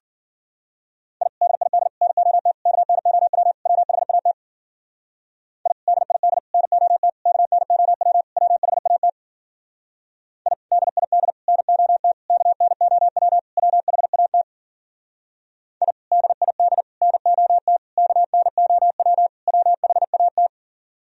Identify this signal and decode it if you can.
Morse code, heard as a keyed tone, about 52 wpm, 700 Hz. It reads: I DID NOT KNOW WHAT I DID NOT KNOW WHAT I DID NOT KNOW WHAT I DID NOT KNOW WHAT